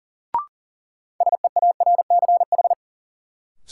Morse code, heard as a keyed tone, about 40 wpm, 700 Hz. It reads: SEARCH
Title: Sentences from Top 800 Words - New Word Drill - 40wpm